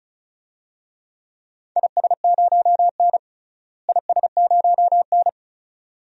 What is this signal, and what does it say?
Morse code, heard as a keyed tone, about 34 wpm, 700 Hz. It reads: IS0N IS0N
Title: DXCC-100 Call Signs - 4 Characters - Repeated 2x - 35wpm